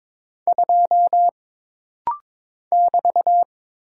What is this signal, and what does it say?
Morse code, heard as a keyed tone, about 22 wpm, 700 Hz. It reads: 2 <BT>
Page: This rapid-fire timing is geared to push your ICR (Instant Character Recognition) skills.